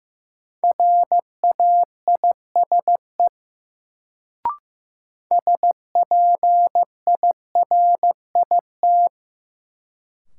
Morse code, heard as a keyed tone, about 15 wpm, 700 Hz. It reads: RAISE SPIRIT
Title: Sentences from Top 1400 Words - New Word Drill - 15wpm